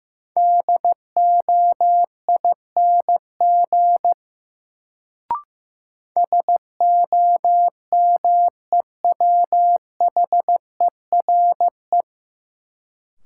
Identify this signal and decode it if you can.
Morse code, heard as a keyed tone, about 15 wpm, 700 Hz. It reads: DOING SOMEWHERE
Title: Sentences from Top 1400 Words - New Word Drill - 15wpm